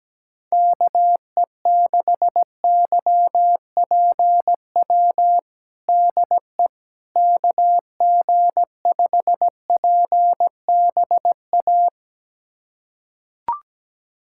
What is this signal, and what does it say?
Morse code, heard as a keyed tone, about 17 wpm, 700 Hz. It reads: KE6YPW DE KG5PBA
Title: USA Callsign DE Callsign - 6 Characters - 17wpm